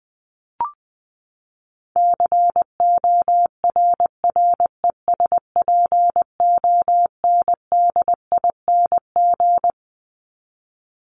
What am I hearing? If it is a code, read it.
Morse code, heard as a keyed tone, about 20 wpm, 700 Hz. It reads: CORRESPONDING
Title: ING Suffix - Top 100 Words with Base Words 20wpm